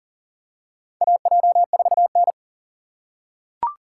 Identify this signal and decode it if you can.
Morse code, heard as a keyed tone, about 40 wpm, 700 Hz. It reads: AJ4N
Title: USA Call Signs - 3x Character Spacing - 40wpm